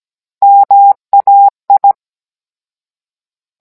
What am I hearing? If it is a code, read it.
Morse code, heard as a keyed tone, about 17 wpm, 800 Hz. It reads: MAI
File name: Cours-NR18-16mm